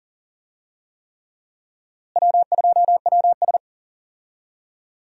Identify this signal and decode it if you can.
Morse code, heard as a keyed tone, about 40 wpm, 700 Hz. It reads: W2WS
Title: USA Call Signs - 4 Characters - Repeated 3x - 40wpm